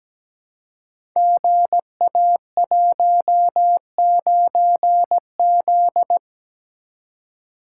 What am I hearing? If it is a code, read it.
Morse code, heard as a keyed tone, about 17 wpm, 700 Hz. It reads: GA19Z
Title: Code Groups - 5 Letter:Numbers - 17wpm